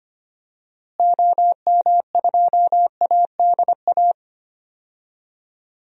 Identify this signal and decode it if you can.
Morse code, heard as a keyed tone, about 25 wpm, 700 Hz. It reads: OM2ADA
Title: DXCC-100 Call Signs - 6 Characters - Repeated 3x - 25wpm